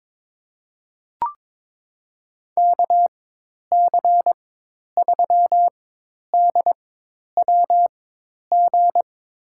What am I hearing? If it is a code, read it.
Morse code, heard as a keyed tone, about 23 wpm, spaced as for 10 wpm, 700 Hz. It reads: KC3DWG
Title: USA Call Signs - 6 Characters - 4x Character Spacing - 22wpm